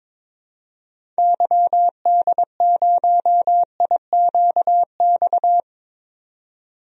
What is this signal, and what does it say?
Morse code, heard as a keyed tone, about 22 wpm, 700 Hz. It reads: YD0IQX
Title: DXCC-100 Call Signs - 6 Characters - 22wpm